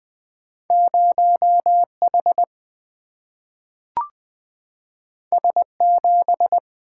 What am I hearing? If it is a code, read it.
Morse code, heard as a keyed tone, about 20 wpm, 700 Hz. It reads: 0H S7
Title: Single-Letter Single-Number 20wpm